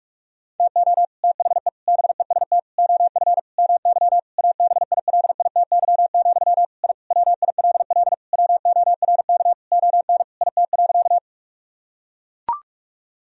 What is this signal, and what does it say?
Morse code, heard as a keyed tone, about 41 wpm, 700 Hz. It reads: TO THE BEST OF MY ABILITY, I WILL WORK ON IT.
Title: Sentences from Top 1000 Words - Review Part 3 - 45wpm